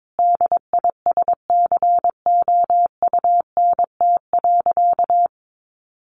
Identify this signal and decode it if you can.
Morse code, heard as a keyed tone, about 22 wpm, 700 Hz. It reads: DISCOUNT.
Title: Sentences from Top 1200 Words 22wpm